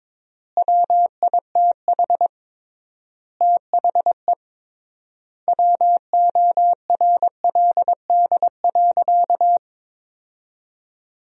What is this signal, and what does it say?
Morse code, heard as a keyed tone, about 21 wpm, 700 Hz. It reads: WITH THE WORLD.
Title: 3x Word-Spacing - Sentences from Top 200 Words - 22wpm